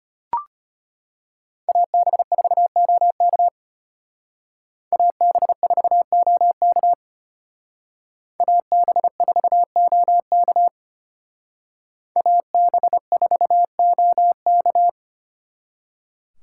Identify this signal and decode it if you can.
Morse code, heard as a keyed tone, about 31 wpm, 700 Hz. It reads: AB4OK AB4OK AB4OK AB4OK
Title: Speed Racing - USA 2x2 Call Signs 25wpm